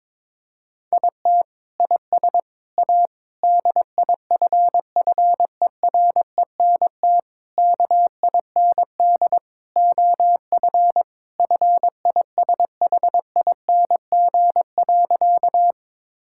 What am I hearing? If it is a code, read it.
Morse code, heard as a keyed tone, about 22 wpm, 700 Hz. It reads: IT IS A DIFFERENT KIND OF FISHING.
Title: Sentences from Top 1000 Words - Review Part 5 - 22wpm